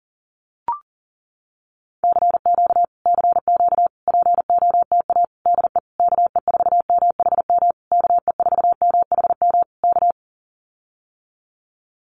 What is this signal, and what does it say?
Morse code, heard as a keyed tone, about 40 wpm, 700 Hz. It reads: CQ CQ POTA DE KE4MHM KE4MHM K
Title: Common QSO Phrases Encore 40wpm